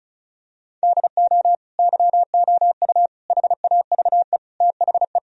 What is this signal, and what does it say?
Morse code, heard as a keyed tone, about 34 wpm, 700 Hz. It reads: DO YOU HAVE THE
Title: Sentences from Top 1000 Words - Review Part 2 - 35wpm